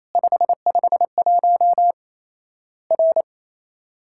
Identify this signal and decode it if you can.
Morse code, heard as a keyed tone, about 28 wpm, 700 Hz. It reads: 551 R
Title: Worked All Europe DX Contest - QTC Exchange - 28wpm